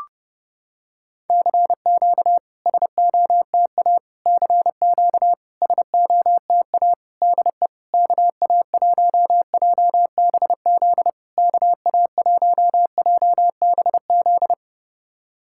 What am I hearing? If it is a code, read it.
Morse code, heard as a keyed tone, about 30 wpm, 690 Hz. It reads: CQ SOTA CQ SOTA DE KA1JBZ KA1JBZ
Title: SOTA QSO Practice 30wpm